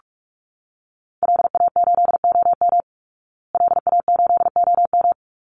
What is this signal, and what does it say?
Morse code, heard as a keyed tone, about 44 wpm, 700 Hz. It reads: LA8OM LA8OM
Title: Call Signs Repeated 3x - 45wpm